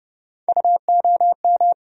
Morse code, heard as a keyed tone, about 30 wpm, 700 Hz. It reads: UOM
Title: USA Call Sign Postfixes - 3 Letters 30wpm